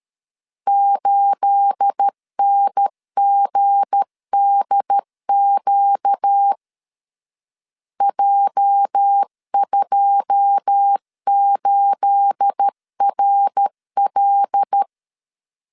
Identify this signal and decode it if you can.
Morse code, heard as a keyed tone, about 13 wpm, 800 Hz. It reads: 8NGDQ J28RL
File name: LEZIONI / 9 28 12wpm 2351